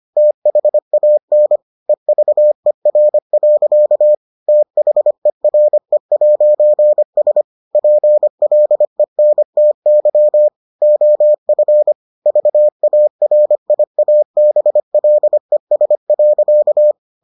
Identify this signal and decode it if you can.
Morse code, heard as a keyed tone, about 25 wpm, 600 Hz. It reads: THAN EVER. THERE'S PLENTY OF VARIABLES.